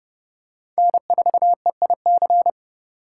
Morse code, heard as a keyed tone, about 30 wpm, 700 Hz. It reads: N4EIC
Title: USA Call Signs - 5 Characters - 3x Character Spacing - 30wpm